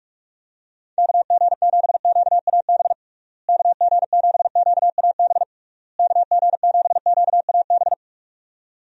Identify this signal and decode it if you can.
Morse code, heard as a keyed tone, about 46 wpm, 700 Hz. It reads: KG7QAB KG7QAB KG7QAB
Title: USA Call Signs - 6 Characters - Repeated 3x - 45wpm